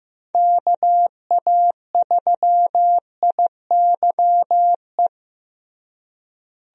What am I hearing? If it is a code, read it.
Morse code, heard as a keyed tone, about 15 wpm, 700 Hz. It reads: KA3IYE
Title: USA Call Signs - 6 Characters - Repeated 2x - 15wpm